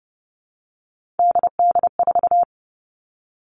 Morse code, heard as a keyed tone, about 30 wpm, 700 Hz. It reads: DD4
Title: All Cut Numbers - 3-Digit 30wpm